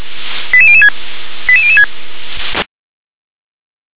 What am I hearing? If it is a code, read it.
Five-tone selective call (ZVEI), heard as quick runs of five tones, tones of about 70 ms. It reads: 88006 88006
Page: /357/SDR/ZVEI/88159/